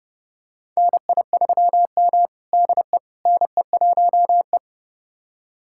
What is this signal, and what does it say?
Morse code, heard as a keyed tone, about 30 wpm, 700 Hz. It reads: NI3M DE NE1E
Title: USA Callsign DE Callsign - 4 Characters - 30wpm